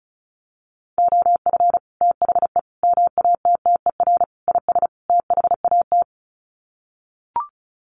Morse code, heard as a keyed tone, about 34 wpm, 700 Hz. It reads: OF THE MATTER IS THAT
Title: Sets of 5 Words Encore 35wpm